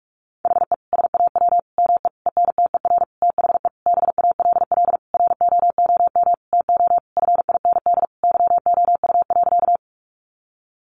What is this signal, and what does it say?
Morse code, heard as a keyed tone, about 44 wpm, 700 Hz. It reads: HE SAW ME ENTER THE BALL ROOM TO FIND YOU.
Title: Sentences from Top 1300 Words 45wpm